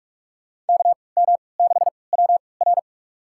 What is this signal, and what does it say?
Morse code, heard as a keyed tone, about 46 wpm, spaced as for 25 wpm, 700 Hz. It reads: KM6WR
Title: USA Call Signs - 3x Character Spacing - 45wpm